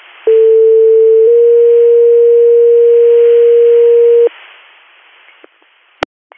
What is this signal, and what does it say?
Two-tone page, heard as a one-second tone, then a three-second tone, about 450 then 470 Hz. It is heard from 0.3 to 4.3 s.